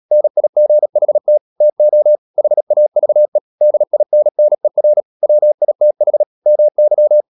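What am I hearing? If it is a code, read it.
Morse code, heard as a keyed tone, about 37 wpm, 600 Hz. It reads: NIGHT TO HAVE DINNER WITH MY
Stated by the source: Morse practice at 36 wpm, text of Ken Follett's novel Never, from CW med Gnister.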